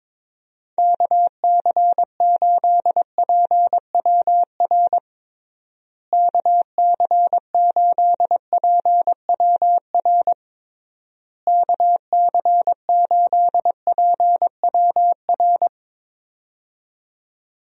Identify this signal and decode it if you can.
Morse code, heard as a keyed tone, about 21 wpm, 700 Hz. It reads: KC8PWR KC8PWR KC8PWR
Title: USA Call Signs - 6 Characters - Repeated 3x - 22wpm